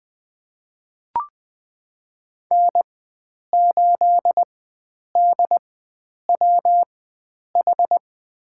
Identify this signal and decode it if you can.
Morse code, heard as a keyed tone, about 20 wpm, spaced as for 9 wpm, 700 Hz. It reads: N8DWH
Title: USA Call Signs - 5 Characters - 4x Character Spacing - 20wpm